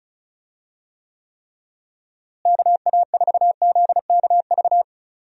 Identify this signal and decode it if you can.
Morse code, heard as a keyed tone, about 36 wpm, 700 Hz. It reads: KA4ZKV
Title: USA Call Signs - 6 Characters - Repeated 3x - 35wpm